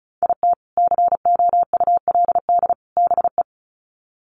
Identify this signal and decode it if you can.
Morse code, heard as a keyed tone, about 34 wpm, 700 Hz. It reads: IT COULD BE
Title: Sets of 3 Words 35wpm